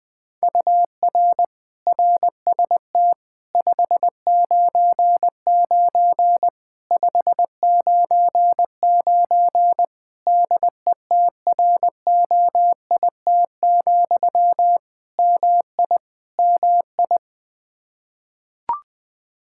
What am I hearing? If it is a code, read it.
Morse code, heard as a keyed tone, about 21 wpm, 700 Hz. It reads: UR RST 599 599 DETROIT, MI MI
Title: Practice QSO - Exchange 1 - Beginning - 20wpm